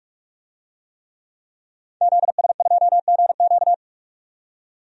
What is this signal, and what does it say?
Morse code, heard as a keyed tone, about 45 wpm, 700 Hz. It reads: ZS2GQ
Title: DXCC-100 Call Signs - 5 Characters - 45wpm